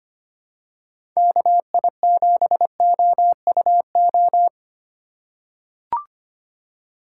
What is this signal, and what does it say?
Morse code, heard as a keyed tone, about 25 wpm, 700 Hz. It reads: KI7OUO
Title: USA Call Signs - 6 Characters - 4x Character Spacing - 25wpm